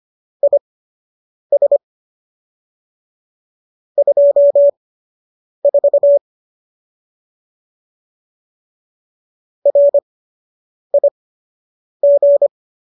Morse code, heard as a keyed tone, about 25 wpm, spaced as for 8 wpm, 570 Hz. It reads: IS 24 RIG